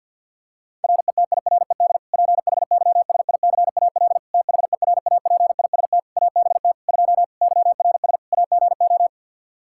Morse code, heard as a keyed tone, about 50 wpm, 700 Hz. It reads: RETIRED PHYSICAL THERAPIST ABT 2 YRS AGO
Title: K1USN SST QSO Practice 50wpm